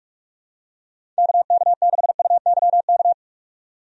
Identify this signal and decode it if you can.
Morse code, heard as a keyed tone, about 46 wpm, 700 Hz. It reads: KK6UYK
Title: USA Call Signs - 6 Characters - 45wpm